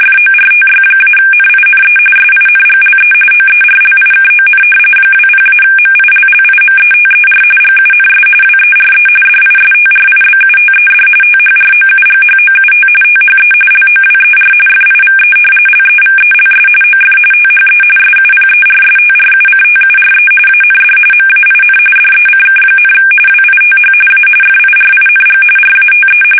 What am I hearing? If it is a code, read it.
RTTY, about 45.45 baud. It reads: FORTY-NINE SINGLE OERLIKON 20-MILLIMETER GUNS WERE DISTRIBUTED THROUGHOUT THE LOWER FIVE LEVELS OF THE SUPERSTRUCTURE AND MAIN DECK, WITH A FIRING RATE
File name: fsk850 / battleshipnj